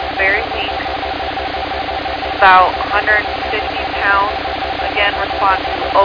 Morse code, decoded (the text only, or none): none